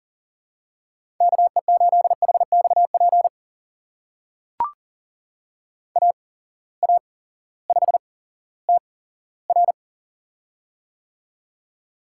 Morse code, heard as a keyed tone, about 38 wpm, spaced as for 18 wpm, 700 Hz. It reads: KE8HXP AA5TR
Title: USA Call Signs - 8x Character Spacing - 40wpm